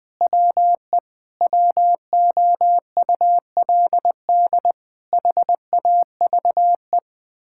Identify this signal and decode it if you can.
Morse code, heard as a keyed tone, about 20 wpm, 700 Hz. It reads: WE WOULD HAVE
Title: Sentences with 7-Word Phrases - Spoken Beginning - 20wpm